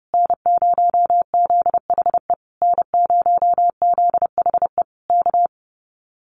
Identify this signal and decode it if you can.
Morse code, heard as a keyed tone, about 30 wpm, 700 Hz. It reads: N0ZHE N0ZHE K
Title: Common QSO Phrases Encore 30wpm